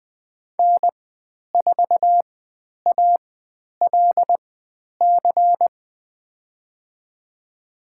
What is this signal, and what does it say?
Morse code, heard as a keyed tone, about 20 wpm, spaced as for 10 wpm, 700 Hz. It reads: N4ALC
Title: 109 Lesson 18g - Letter m - Call Signs 20wpm